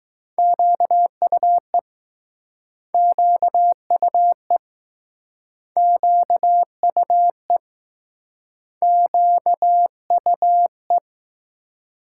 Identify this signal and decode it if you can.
Morse code, heard as a keyed tone, about 20 wpm, 700 Hz. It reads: QUE QUE QUE QUE